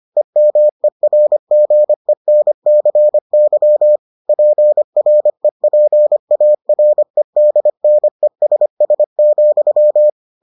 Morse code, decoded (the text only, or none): EMERGENCY PREPAREDNESS,